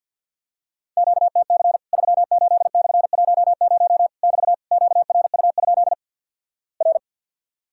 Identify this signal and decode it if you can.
Morse code, heard as a keyed tone, about 51 wpm, 700 Hz. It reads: QTC 38/10 <BT> QRU? R
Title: Worked All Europe DX Contest - QTC Exchange with Repeats - 50wpm